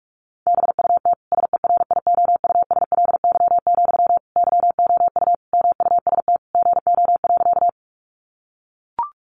Morse code, heard as a keyed tone, about 44 wpm, 700 Hz. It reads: BUT SERIOUSLY, YOU MUST GO.
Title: Sentences from Top 1200 Words 45wpm